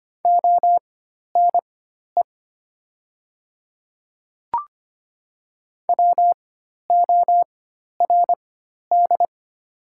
Morse code, heard as a keyed tone, about 25 wpm, spaced as for 12 wpm, 690 Hz. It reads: ONE WORD
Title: 4x Character Spacing - Top 300 Words - 25wpm